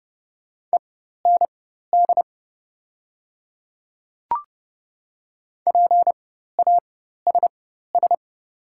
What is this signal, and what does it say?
Morse code, heard as a keyed tone, about 31 wpm, spaced as for 14 wpm, 700 Hz. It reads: END PASS
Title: 4x Character Spacing - Top 500 Words - 30wpm